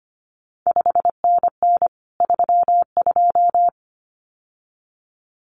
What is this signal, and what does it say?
Morse code, heard as a keyed tone, about 25 wpm, 700 Hz. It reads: 5NN 32
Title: CQ WW DX Contest Exchange Encore - 25wpm